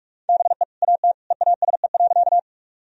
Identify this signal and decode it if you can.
Morse code, heard as a keyed tone, about 46 wpm, 700 Hz. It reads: BE AT EASE.
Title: Sentences from Top 1000 Words - Review Part 1 - 45wpm